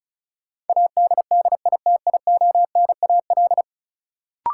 Morse code, heard as a keyed tone, about 36 wpm, 700 Hz. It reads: ADDITIONAL
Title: Sentences from Top 1400 Words - New Word Drill Alt - 35wpm